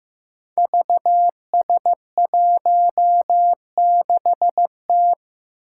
Morse code, heard as a keyed tone, about 15 wpm, 700 Hz. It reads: VS16T
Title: Five Challenging Letter-Number Combinations 15wpm